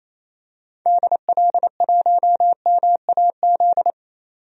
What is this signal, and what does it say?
Morse code, heard as a keyed tone, about 28 wpm, 700 Hz. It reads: DL1MAZ
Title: DXCC-100 Call Signs - 6 Characters - 28wpm